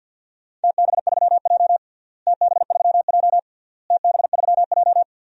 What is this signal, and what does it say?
Morse code, heard as a keyed tone, about 48 wpm, 700 Hz. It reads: TB3J TB3J TB3J
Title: DXCC-100 Call Signs - 4 Characters - Repeated 3x - 50wpm